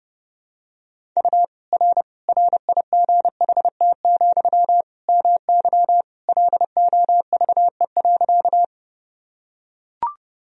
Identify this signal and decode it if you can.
Morse code, heard as a keyed tone, about 29 wpm, 700 Hz. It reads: U R RIGHT, MY LOVE.
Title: Sentences from Top 400 Words and Abbreviations 30wpm